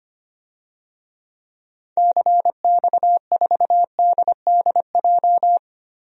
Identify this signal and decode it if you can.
Morse code, heard as a keyed tone, about 25 wpm, 700 Hz. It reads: CX4DDJ
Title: DXCC-100 Call Signs - 6 Characters - 25wpm